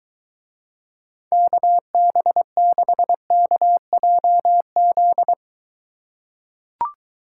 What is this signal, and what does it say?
Morse code, heard as a keyed tone, about 23 wpm, 700 Hz. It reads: KB6KJZ